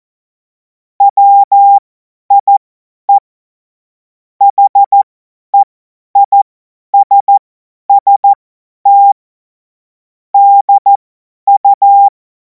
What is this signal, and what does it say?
Morse code, heard as a keyed tone, about 14 wpm, spaced as for 10 wpm, 800 Hz. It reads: WIE HEISST DU
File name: Morsekurs / Witze / Witz 10. März 2026 50 F70